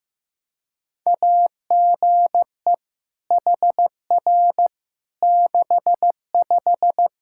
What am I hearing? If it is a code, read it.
Morse code, heard as a keyed tone, about 15 wpm, 700 Hz. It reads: AGE HR 65
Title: Practice QSO - Exchange 3 - More Info - 15wpm